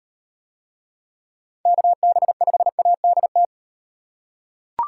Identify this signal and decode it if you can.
Morse code, heard as a keyed tone, about 38 wpm, 700 Hz. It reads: KB5ADT